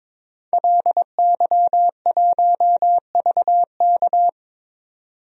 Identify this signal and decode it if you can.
Morse code, heard as a keyed tone, about 21 wpm, 700 Hz. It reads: LY1VK
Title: DXCC-100 Call Signs - 5 Characters - 22wpm